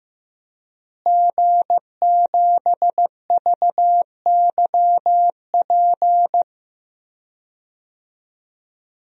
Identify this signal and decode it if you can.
Morse code, heard as a keyed tone, about 15 wpm, 700 Hz. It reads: G7VYP
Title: DXCC-100 Call Signs - 5 Characters - Repeated 2x - 15wpm